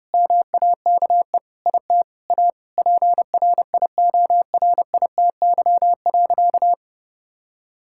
Morse code, heard as a keyed tone, about 32 wpm, 700 Hz. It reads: MAKE IT A PRIORITY.
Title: Sentences from Top 1600 Words 30wpm